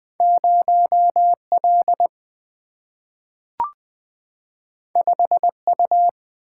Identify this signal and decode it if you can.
Morse code, heard as a keyed tone, about 20 wpm, 690 Hz. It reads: 0L 5U
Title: Two Letter-Number Combinations - 20wpm